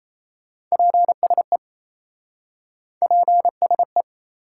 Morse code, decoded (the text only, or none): PSE PSE